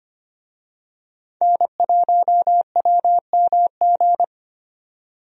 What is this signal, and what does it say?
Morse code, heard as a keyed tone, about 25 wpm, 690 Hz. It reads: N1WMG
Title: USA Call Signs - 5 Characters - 25wpm